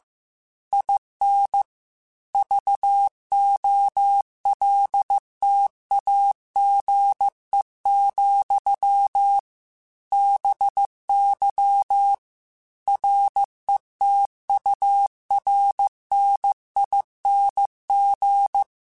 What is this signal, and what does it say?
Morse code, heard as a keyed tone, about 15 wpm, 800 Hz. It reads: IN VOLTAGE, BY RETURNING